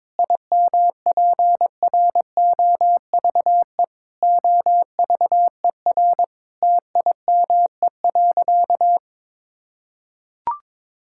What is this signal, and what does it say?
Morse code, heard as a keyed tone, about 22 wpm, 700 Hz. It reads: IMPROVE OVER TIME.
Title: Sentences from Top 1000 Words - Review Part 5 - 22wpm